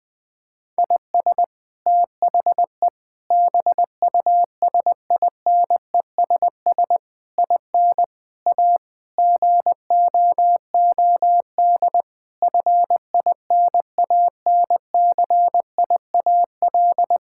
Morse code, heard as a keyed tone, about 20 wpm, 700 Hz. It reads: IS THE BUSINESS IN A GOOD FINANCIAL
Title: Sentences from Top 1000 Words - Review Part 5 - 20wpm